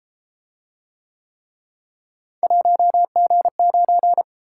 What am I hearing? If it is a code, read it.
Morse code, heard as a keyed tone, about 30 wpm, 700 Hz. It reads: 1G9